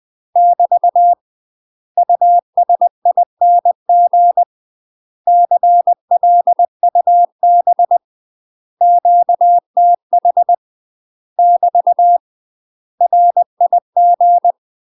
Morse code, heard as a keyed tone, about 20 wpm, 700 Hz. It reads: <BT> USING CLUB QTH <BT> RIG